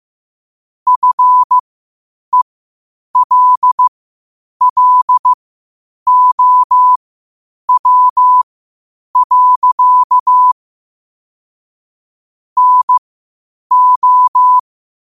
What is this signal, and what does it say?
Morse code, heard as a keyed tone, about 15 wpm, spaced as for 8 wpm, 1000 Hz. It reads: FELLOW. NO